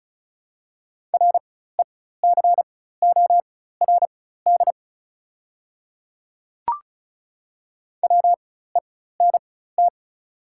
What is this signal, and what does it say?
Morse code, heard as a keyed tone, about 36 wpm, spaced as for 16 wpm, 690 Hz. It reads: RECORD WENT
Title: 4x Character Spacing - Top 500 Words - 35wpm